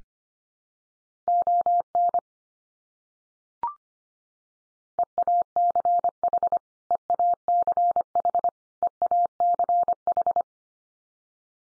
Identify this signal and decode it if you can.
Morse code, heard as a keyed tone, about 25 wpm, 700 Hz. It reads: ON EACH EACH EACH
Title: Top 100 Words 25wpm x3